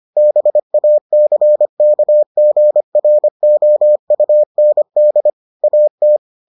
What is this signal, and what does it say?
Morse code, heard as a keyed tone, about 25 wpm, 600 Hz. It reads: BACKGROUND AT